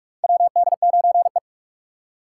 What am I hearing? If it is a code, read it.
Morse code, heard as a keyed tone, about 46 wpm, 700 Hz. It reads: WD9E
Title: USA Call Signs - 4 Characters - 3x Character Spacing - 45wpm